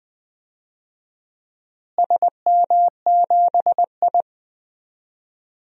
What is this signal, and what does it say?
Morse code, heard as a keyed tone, about 20 wpm, 700 Hz. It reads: SM7I
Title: DXCC-100 Call Signs - 4 Characters - Repeated 2x - 20wpm